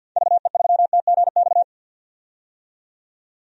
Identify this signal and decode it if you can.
Morse code, heard as a keyed tone, about 52 wpm, 700 Hz. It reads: VE3TGX
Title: DXCC-100 Call Signs - 6 Characters - 50wpm